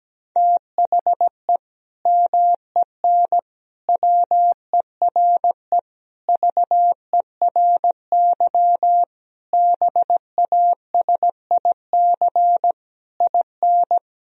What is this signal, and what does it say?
Morse code, heard as a keyed tone, about 17 wpm, 700 Hz. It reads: THE MEN WERE VERY BASIC IN
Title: Sentences from Top 1000 Words - Review Part 3 - 17wpm